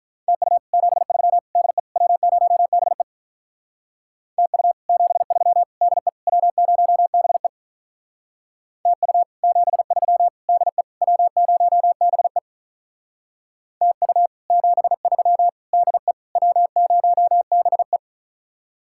Morse code, heard as a keyed tone, about 45 wpm, 700 Hz. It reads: TU 73 DE W0BE TU 73 DE W0BE TU 73 DE W0BE TU 73 DE W0BE